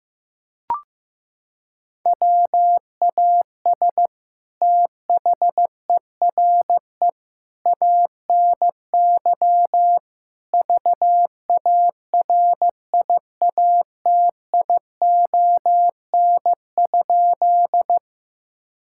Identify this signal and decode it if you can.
Morse code, heard as a keyed tone, about 15 wpm, 700 Hz. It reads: WAS THERE ANY VARIATION?
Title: Sentences from Top 1600 Words 15wpm